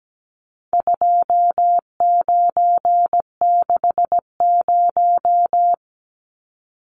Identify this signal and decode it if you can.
Morse code, heard as a keyed tone, about 17 wpm, 700 Hz. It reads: 2960
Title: Numbers - 4-Digit 17wpm